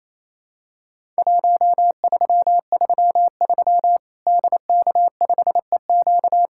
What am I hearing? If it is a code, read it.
Morse code, heard as a keyed tone, about 28 wpm, 700 Hz. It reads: 1333 DK5EQ
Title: Worked All Europe DX Contest - QTC Exchange - 28wpm